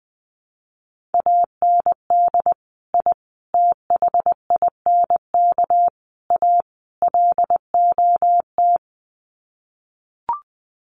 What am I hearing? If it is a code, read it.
Morse code, heard as a keyed tone, about 19 wpm, 700 Hz. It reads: AND I THINK A LOT
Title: Sets of 5 Words Encore2 20wpm